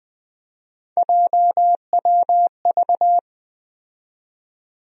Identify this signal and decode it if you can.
Morse code, heard as a keyed tone, about 21 wpm, 700 Hz. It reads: JWV
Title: USA Call Sign Postfixes - 3 Letters 20wpm